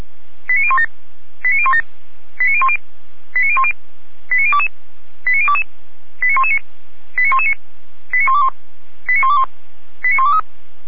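Five-tone selective call (ZVEI), heard as quick runs of five tones, tones of about 70 ms. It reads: 89017 89017 89010 89010 89022 89022 89119 89119 89121 89121 89123
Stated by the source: Five-tone ZVEI selective call received on an SDR.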